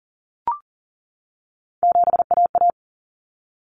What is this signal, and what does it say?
Morse code, heard as a keyed tone, about 40 wpm, 700 Hz. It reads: 7AA
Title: Common Cut Numbers - 3-Digit 40wpm